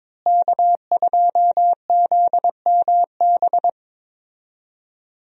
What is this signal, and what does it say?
Morse code, heard as a keyed tone, about 23 wpm, 700 Hz. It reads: K2ZMB
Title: USA Call Signs - 5 Characters - Repeated 3x - 22wpm